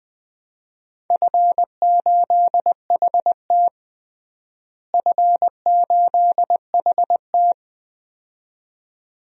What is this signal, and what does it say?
Morse code, heard as a keyed tone, about 20 wpm, 700 Hz. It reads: F8HT F8HT
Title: DXCC-100 Call Signs - 4 Characters - Repeated 2x - 20wpm